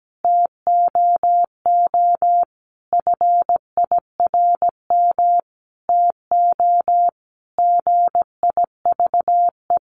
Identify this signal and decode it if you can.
Morse code, heard as a keyed tone, about 17 wpm, 700 Hz. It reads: TOO FIRM TO GIVE
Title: Sentences from Top 1400 Words 17wpm